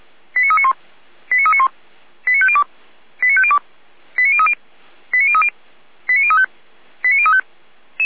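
Five-tone selective call (ZVEI), heard as quick runs of five tones, tones of about 70 ms. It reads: 89391 89391 89502 89502 89030 89030 89035 89035
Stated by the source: ZVEI five-tone selective call received on an SDR.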